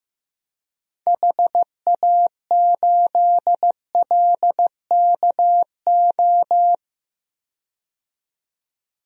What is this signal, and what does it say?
Morse code, heard as a keyed tone, about 15 wpm, 700 Hz. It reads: HA8LKO
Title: DXCC-100 Call Signs - 6 Characters - 15wpm